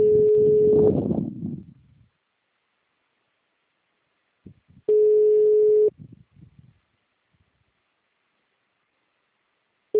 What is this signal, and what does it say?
Ringback tone (Europe, 425 Hz).
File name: out- -2001- - - .407